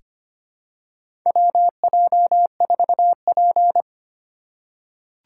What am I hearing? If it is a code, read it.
Morse code, heard as a keyed tone, about 25 wpm, 690 Hz. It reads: WJ4P
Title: USA Call Signs - 4 Characters - 4x Character Spacing - 25wpm